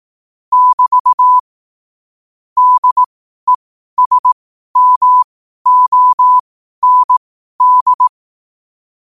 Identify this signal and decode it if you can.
Morse code, heard as a keyed tone, about 18 wpm, spaced as for 12 wpm, 1000 Hz. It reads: <BT> DESMOND
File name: qotdfiles / 12wpm